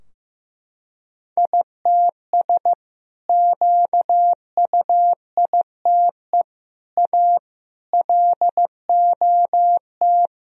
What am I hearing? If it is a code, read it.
Morse code, heard as a keyed tone, about 15 wpm, 700 Hz. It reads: ITS QUITE A LOT
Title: Sentences from Top 1000 Words - Review Part 3 - 15wpm